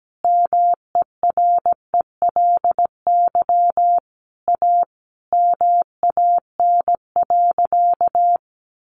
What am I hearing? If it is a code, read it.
Morse code, heard as a keyed tone, about 17 wpm, 700 Hz. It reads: MERELY A MAN.
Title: Sentences from Top 1400 Words 17wpm